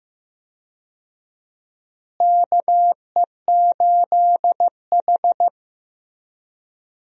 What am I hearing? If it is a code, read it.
Morse code, heard as a keyed tone, about 15 wpm, 700 Hz. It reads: KE8H
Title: USA Call Signs - 4 Characters - Repeated 3x - 15wpm